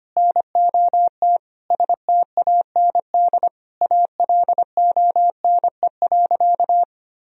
Morse code, heard as a keyed tone, about 25 wpm, 700 Hz. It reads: NOT STAND ALONE.
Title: Sentences from Top 1000 Words - Review Part 3 - 25wpm